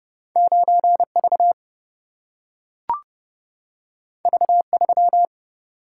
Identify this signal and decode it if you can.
Morse code, heard as a keyed tone, about 30 wpm, 700 Hz. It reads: 9V V3
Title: Single-Letter Single-Number 30wpm